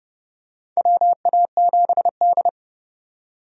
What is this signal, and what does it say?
Morse code, heard as a keyed tone, about 30 wpm, 700 Hz. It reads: WA7D
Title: USA Call Signs - 4 Characters - 30wpm